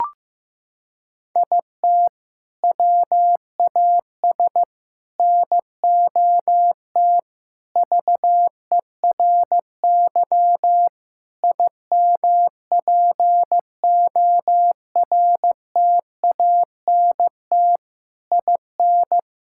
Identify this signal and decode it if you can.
Morse code, heard as a keyed tone, about 15 wpm, 700 Hz. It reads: IT WAS NOT VERY IMPORTANT IN
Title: Sentences from Top 1000 Words - Review Part 5 - 15wpm